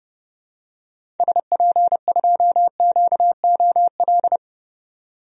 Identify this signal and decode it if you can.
Morse code, heard as a keyed tone, about 30 wpm, 700 Hz. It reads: SP2QOL
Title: DXCC-100 Call Signs - 6 Characters - 30wpm